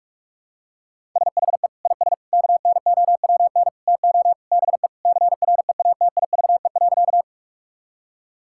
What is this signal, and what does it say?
Morse code, heard as a keyed tone, about 41 wpm, 700 Hz. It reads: SHE IS KNOWN TO BE CREATIVE.
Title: Sentences from Top 1000 Words - Review Part 5 - 45wpm